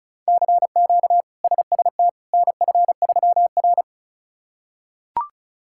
Morse code, heard as a keyed tone, about 36 wpm, 700 Hz. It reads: CQ SST NF3R
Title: K1USN SST QSO Practice - 35wpm